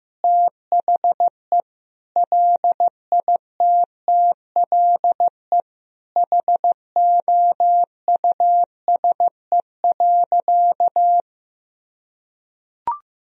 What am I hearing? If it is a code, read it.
Morse code, heard as a keyed tone, about 16 wpm, 700 Hz. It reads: THE LITTLE HOUSE.
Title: Sentences from Top 200 Words and Abbreviations 15wpm